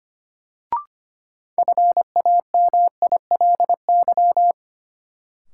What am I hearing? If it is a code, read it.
Morse code, heard as a keyed tone, about 25 wpm, 700 Hz. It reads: FAMILY
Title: Sentences from Top 400 Words - New Word Drill - 25wpm